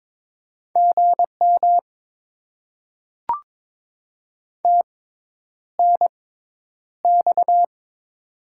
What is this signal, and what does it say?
Morse code, heard as a keyed tone, about 22 wpm, spaced as for 10 wpm, 700 Hz. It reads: GM TNX
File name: Top 20 QSO Elements - 6x Character Spacing - 22wpm